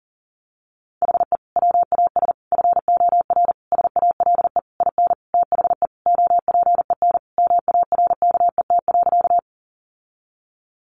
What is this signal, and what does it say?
Morse code, heard as a keyed tone, about 40 wpm, 700 Hz. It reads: HE WAS FOR SALE IN THE OPEN MARKET.
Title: Sentences from Top 1400 Words 40wpm